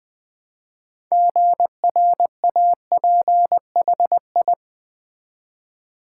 Morse code, heard as a keyed tone, about 20 wpm, 700 Hz. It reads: GRAPHI
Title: Build A Word - 2 Letters At A Time - 8-Letter Words - 20wpm